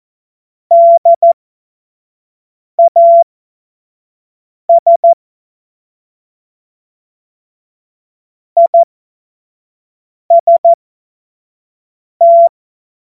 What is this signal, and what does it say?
Morse code, heard as a keyed tone, about 14 wpm, spaced as for 5 wpm, 680 Hz. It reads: DAS IST
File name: Morsekurs / Witze / Witz 27. Juni 2023 25 F70